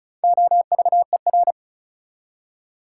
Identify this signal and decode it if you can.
Morse code, heard as a keyed tone, about 34 wpm, 700 Hz. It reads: OVER
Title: Prefixes and Suffixes 35wpm